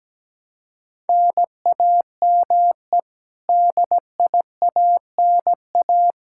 Morse code, heard as a keyed tone, about 17 wpm, 700 Hz. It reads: NAME DIANA
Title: Practice QSO - Exchange 1 - Beginning - 17wpm